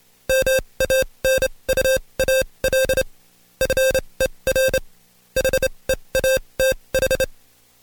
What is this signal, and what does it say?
Morse code, heard as a keyed tone, about 27 wpm, spaced as for 22 wpm, 530 Hz. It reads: MANUAL FER HEATH
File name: Code Quick / AudioDownld2 / Seg137-Stepper10 21WPM